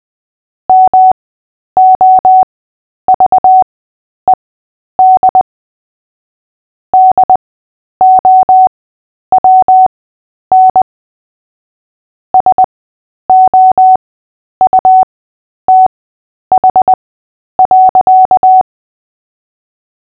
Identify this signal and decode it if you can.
Morse code, heard as a keyed tone, about 20 wpm, spaced as for 10 wpm, 750 Hz. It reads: MOVED DOWN SOUTH.
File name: Short-Story-5-10wpm